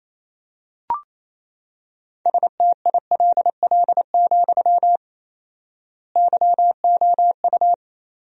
Morse code, heard as a keyed tone, about 27 wpm, 700 Hz. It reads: STILL, YOU
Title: 4x Word-Spacing - Sentences from Top 300 Words - 28wpm